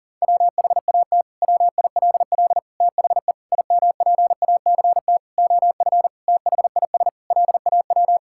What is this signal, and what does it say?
Morse code, heard as a keyed tone, about 37 wpm, 700 Hz. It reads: WHAT WILL THE IMPACT OF THIS LAW
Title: Sentences from Top 1000 Words - Review Part 6 - 40wpm